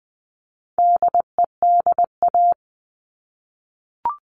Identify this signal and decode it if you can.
Morse code, heard as a keyed tone, about 20 wpm, 700 Hz. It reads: DEDA
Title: All Cut Numbers - 4-Digit 20wpm